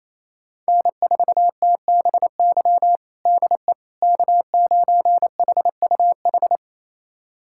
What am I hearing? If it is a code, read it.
Morse code, heard as a keyed tone, about 28 wpm, 700 Hz. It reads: N4TBY DE K9HUH
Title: USA Callsign DE Callsign - 5 Characters - 28wpm